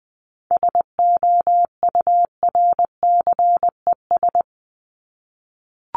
Morse code, heard as a keyed tone, about 20 wpm, 700 Hz. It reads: SOURCES
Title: ES:S Suffix - Top 100 Words with Base Words 20wpm